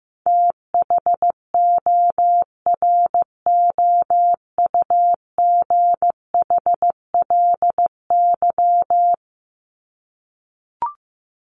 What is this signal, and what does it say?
Morse code, heard as a keyed tone, about 15 wpm, 700 Hz. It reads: THOROUGHLY
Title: LY Suffix - Top 100 Words with Base Words 15wpm